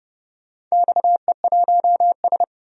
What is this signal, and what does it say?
Morse code, heard as a keyed tone, about 30 wpm, 700 Hz. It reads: XE1S
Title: DXCC-100 Call Signs - 4 Characters - 30wpm